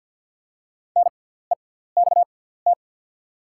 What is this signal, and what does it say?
Morse code, heard as a keyed tone, about 48 wpm, spaced as for 17 wpm, 700 Hz. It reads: NEXT
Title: 6x Character Spacing - Top 500 Words - 50wpm